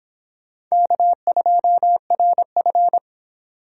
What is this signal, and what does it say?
Morse code, heard as a keyed tone, about 23 wpm, 700 Hz. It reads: K2RF